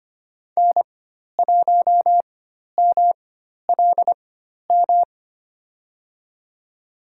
Morse code, heard as a keyed tone, about 25 wpm, spaced as for 12 wpm, 700 Hz. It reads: N1MLM
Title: USA Call Signs - 4x Character Spacing - 25wpm